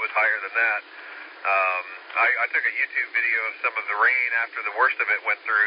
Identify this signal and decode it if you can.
Emergency Alert System (SAME) tones, heard as harsh buzzing bursts, with speech. It is heard from 0.9 to 4.5 s.